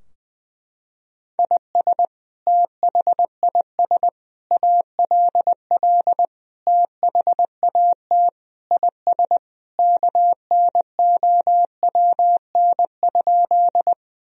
Morse code, heard as a keyed tone, about 21 wpm, 700 Hz. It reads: IS THIS ALL THAT IS KNOWN?
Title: Sentences from Top 1000 Words - Review Part 3 - 20wpm